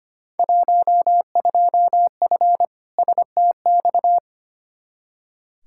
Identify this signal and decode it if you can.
Morse code, heard as a keyed tone, about 25 wpm, 700 Hz. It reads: 12F STX
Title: Field Day Exchanges Encore - 25wpm